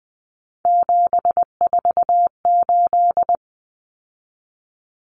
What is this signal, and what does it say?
Morse code, heard as a keyed tone, about 20 wpm, 700 Hz. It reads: 748
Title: Common Cut Numbers - 3-Digit 20wpm